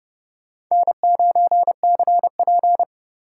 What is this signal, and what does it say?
Morse code, heard as a keyed tone, about 38 wpm, 700 Hz. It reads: N9CP